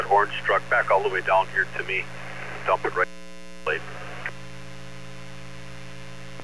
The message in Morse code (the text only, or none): none